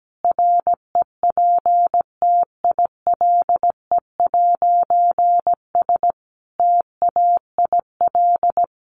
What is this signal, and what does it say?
Morse code, heard as a keyed tone, about 17 wpm, 700 Hz. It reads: REPTILE'S TAIL
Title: A Sound Of Thunder 17wpm